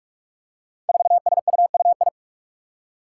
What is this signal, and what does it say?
Morse code, heard as a keyed tone, about 46 wpm, 700 Hz. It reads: 4SUUI
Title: Five Challenging Letter-Number Combinations 45wpm